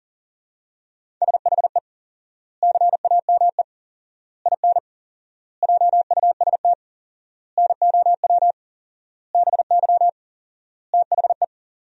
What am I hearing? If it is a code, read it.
Morse code, heard as a keyed tone, about 37 wpm, 700 Hz. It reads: SHE CAME IN JUST NOW BY THE
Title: 4x Word-Spacing - Sentences from Top 300 Words - 40wpm